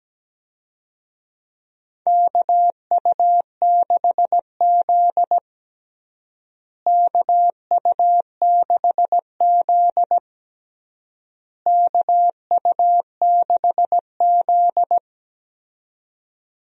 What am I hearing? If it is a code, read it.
Morse code, heard as a keyed tone, about 17 wpm, 700 Hz. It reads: KU6Z KU6Z KU6Z
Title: USA Call Signs - 4 Characters - Repeated 3x - 17wpm